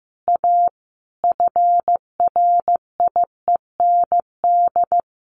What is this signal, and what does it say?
Morse code, heard as a keyed tone, about 15 wpm, 700 Hz. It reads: A FRIEND
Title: Sentences from Top 1400 Words 15wpm